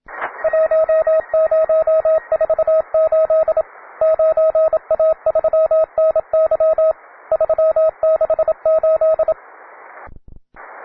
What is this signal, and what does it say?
Morse code, heard as a keyed tone, about 27 wpm, 640 Hz, with radio noise. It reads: 1048 9A3NY 368
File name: wae / cw beginner / ra9ac cw